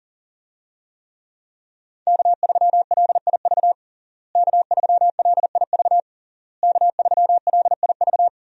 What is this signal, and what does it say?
Morse code, heard as a keyed tone, about 40 wpm, 700 Hz. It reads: K3LIV K3LIV K3LIV
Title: USA Call Signs - 5 Characters - Repeated 3x - 40wpm